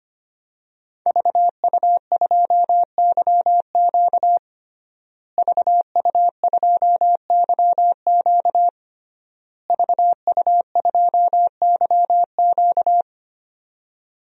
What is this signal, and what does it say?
Morse code, heard as a keyed tone, about 25 wpm, 700 Hz. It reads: VU2YQ VU2YQ VU2YQ
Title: DXCC-100 Call Signs - 5 Characters - Repeated 3x - 25wpm